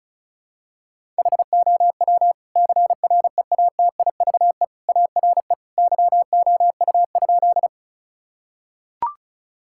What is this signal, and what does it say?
Morse code, heard as a keyed tone, about 34 wpm, 700 Hz. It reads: HOW CREATIVE ARE YOU?
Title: Sentences from Top 1000 Words - Review Part 5 - 35wpm